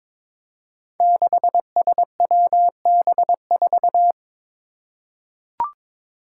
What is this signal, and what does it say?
Morse code, heard as a keyed tone, about 22 wpm, 700 Hz. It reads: 6SWB4
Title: Five Challenging Letter-Number Combinations 22wpm